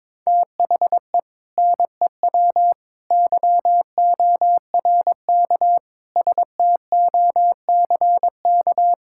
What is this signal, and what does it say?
Morse code, heard as a keyed tone, about 22 wpm, 700 Hz. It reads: THE NEW YORK STOCK
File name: practice / 22wpm / Build A Phrase - 5-Word Phrases - 22wpm